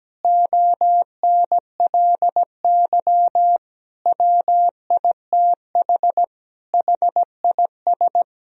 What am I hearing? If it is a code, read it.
Morse code, heard as a keyed tone, about 17 wpm, 700 Hz. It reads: ONLY WITH HIS
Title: Sentences from Top 1000 Words - Review Part 1 - 17wpm